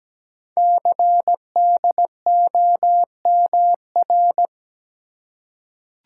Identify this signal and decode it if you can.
Morse code, heard as a keyed tone, about 17 wpm, 700 Hz. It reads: CDOMR
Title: Code Groups - 5 Letters - 17wpm